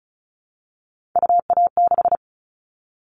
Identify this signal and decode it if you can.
Morse code, heard as a keyed tone, about 34 wpm, 700 Hz. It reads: UA6
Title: Callsign Prefixes for Top 25 DXCC - Repeated 3x - 35wpm